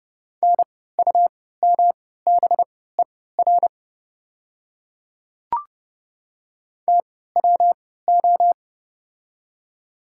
Morse code, heard as a keyed tone, about 31 wpm, spaced as for 17 wpm, 700 Hz. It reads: NUMBER TWO
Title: 3x Character Spacing - Top 100 Words - 30wpm